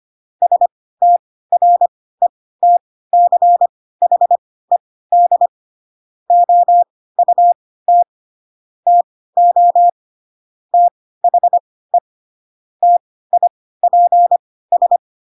Morse code, heard as a keyed tone, about 25 wpm, spaced as for 16 wpm, 700 Hz. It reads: STRETCHED OUT TO THE TIPS